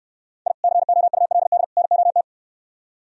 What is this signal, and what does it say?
Morse code, heard as a keyed tone, about 71 wpm, 700 Hz. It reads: I COULD NOT